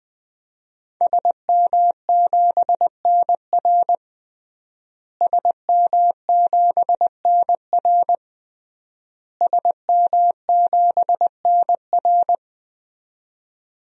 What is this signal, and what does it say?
Morse code, heard as a keyed tone, about 20 wpm, 700 Hz. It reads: SM7NR SM7NR SM7NR
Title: DXCC-100 Call Signs - 5 Characters - Repeated 3x - 20wpm